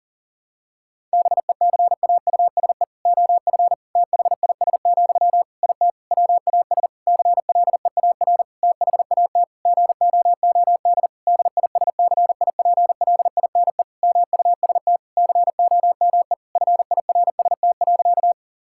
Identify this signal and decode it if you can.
Morse code, heard as a keyed tone, about 40 wpm, 700 Hz. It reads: BECAUSE OF THIS, IT WAS CLEAR THAT GOOD DISCIPLINE MUST COME FIRST.
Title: Sentences from Top 1000 Words - Review Part 6 - 40wpm